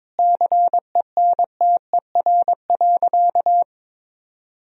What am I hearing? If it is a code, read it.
Morse code, heard as a keyed tone, about 22 wpm, 700 Hz. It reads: CENTER.
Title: Sentences from Top 1000 Words - Review Part 2 - 22wpm